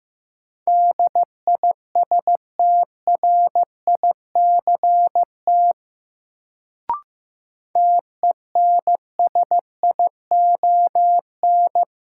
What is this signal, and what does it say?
Morse code, heard as a keyed tone, about 15 wpm, 700 Hz. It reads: DISTRICT TENSION
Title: Sentences from Top 1600 Words - New Word Drill Alt - 15wpm